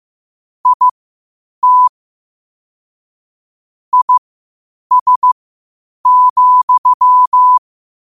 Morse code, heard as a keyed tone, about 15 wpm, spaced as for 8 wpm, 1000 Hz. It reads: IT IS,
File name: qotdfiles / 8wpm